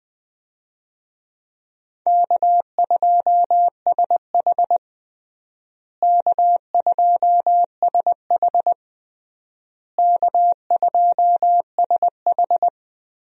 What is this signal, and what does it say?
Morse code, heard as a keyed tone, about 20 wpm, 700 Hz. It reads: K2SH K2SH K2SH
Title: USA Call Signs - 4 Characters - Repeated 3x - 20wpm